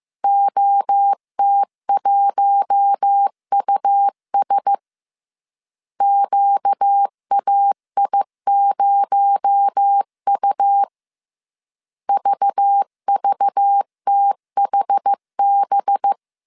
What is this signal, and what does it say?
Morse code, heard as a keyed tone, about 15 wpm, 800 Hz. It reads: OT1US QAI0U VVTHB